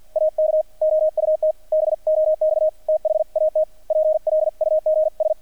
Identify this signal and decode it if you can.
Morse code, heard as a keyed tone, about 50 wpm, 620 Hz, with radio noise. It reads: AK OUT BOX THAT PLUGS
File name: archived / CRSnet